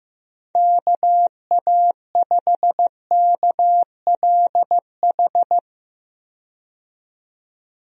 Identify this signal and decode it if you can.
Morse code, heard as a keyed tone, about 15 wpm, 700 Hz. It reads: KA5KLH
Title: USA Call Signs - 6 Characters - 15wpm